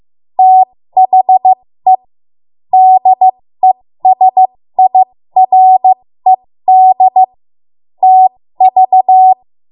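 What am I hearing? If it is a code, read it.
Morse code, heard as a keyed tone, about 15 wpm, spaced as for 13 wpm, 750 Hz. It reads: THE DESIRED TV
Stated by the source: 13 wpm Morse code practice archive transmission.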